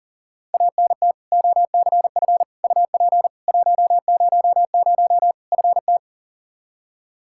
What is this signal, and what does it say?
Morse code, heard as a keyed tone, about 37 wpm, 700 Hz. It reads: ANT OCF UP 100 FT
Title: Practice QSO - Exchange 2 - Info - 40wpm